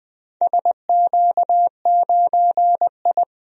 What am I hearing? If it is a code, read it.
Morse code, heard as a keyed tone, about 21 wpm, 700 Hz. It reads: SQ9I
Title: DXCC-100 Call Signs - 4 Characters - 20wpm